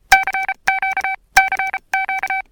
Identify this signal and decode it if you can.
Morse code, heard as a keyed tone, about 32 wpm, spaced as for 28 wpm, 770 Hz. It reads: CQ CQ
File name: PPP RX